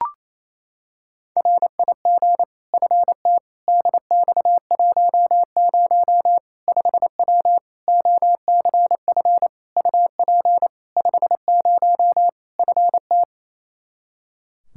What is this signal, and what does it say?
Morse code, heard as a keyed tone, about 28 wpm, 700 Hz. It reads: RIG FT DX10 5W OCF UP 50 FT
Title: Practice QSO - Exchange 2 - Info - 28wpm